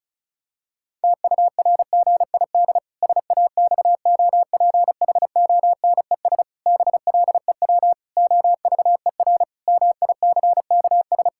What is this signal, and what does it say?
Morse code, heard as a keyed tone, about 34 wpm, 700 Hz. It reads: TURGID SAXOPHONES BLEW OVER MICKS
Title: Pangram Challenge v2 - 35wpm